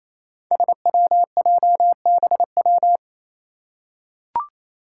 Morse code, heard as a keyed tone, about 28 wpm, 700 Hz. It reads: SWJBW
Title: Five Challenging Letter-Number Combinations 28wpm